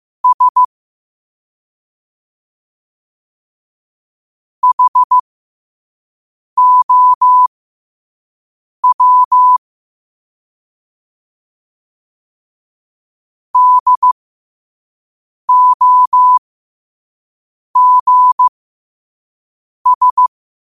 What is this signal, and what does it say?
Morse code, heard as a keyed tone, about 15 wpm, spaced as for 5 wpm, 1000 Hz. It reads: S HOW DOGS
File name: 5wpm